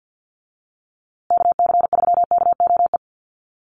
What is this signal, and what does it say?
Morse code, heard as a keyed tone, about 50 wpm, 700 Hz. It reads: KC3KOE
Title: USA Call Signs Repeated 3x 50wpm